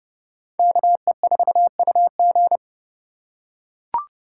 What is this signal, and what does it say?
Morse code, heard as a keyed tone, about 30 wpm, 700 Hz. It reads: KE4UG